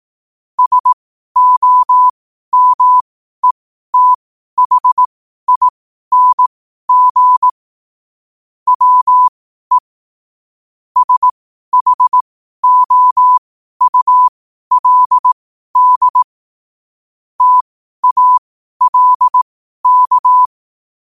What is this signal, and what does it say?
Morse code, heard as a keyed tone, about 18 wpm, spaced as for 12 wpm, 1000 Hz. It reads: SOMETHING WE SHOULD TALK
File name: qotdfiles / 12wpm